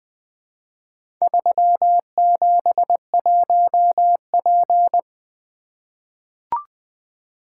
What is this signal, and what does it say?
Morse code, heard as a keyed tone, about 20 wpm, 700 Hz. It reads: 371P
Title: Four Letter-Number Combinations - 20wpm